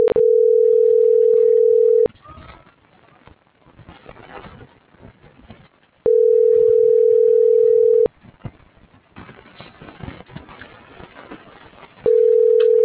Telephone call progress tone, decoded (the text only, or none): ringback tone